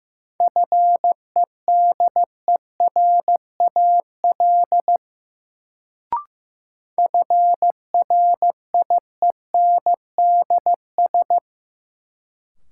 Morse code, heard as a keyed tone, about 15 wpm, 700 Hz. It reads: FEDERAL FRIENDS
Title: Sentences from Top 1000 Words - Review Part 5 - New Word Drill - 15wpm